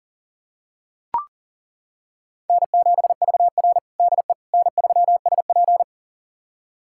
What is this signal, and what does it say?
Morse code, heard as a keyed tone, about 40 wpm, 700 Hz. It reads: N7VR DE N3SP
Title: USA Callsign DE Callsign - 4 Characters - 40wpm